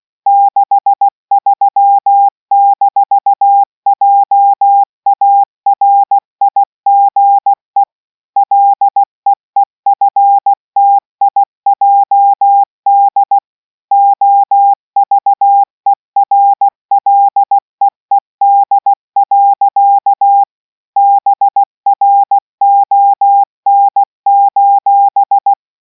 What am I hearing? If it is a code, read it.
Morse code, heard as a keyed tone, about 16 wpm, 800 Hz. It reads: 63-JARIGE LEEFTIJD OVERLEED. BRON: